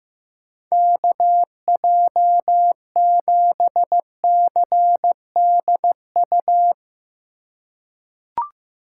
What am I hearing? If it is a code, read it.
Morse code, heard as a keyed tone, about 15 wpm, 700 Hz. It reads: KJ7CDU
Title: USA Call Signs - 6 Characters - 6x Character Spacing - 15wpm